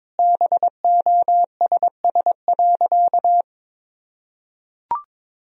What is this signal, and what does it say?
Morse code, heard as a keyed tone, about 22 wpm, 700 Hz. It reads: BOSS.
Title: Sentences from Top 1000 Words - Review Part 4 - 22wpm